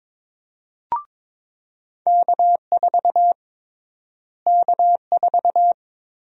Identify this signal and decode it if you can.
Morse code, heard as a keyed tone, about 22 wpm, 700 Hz. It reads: K4 K4
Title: USA Call Sign Prefixes - Repeated 2x - 22wpm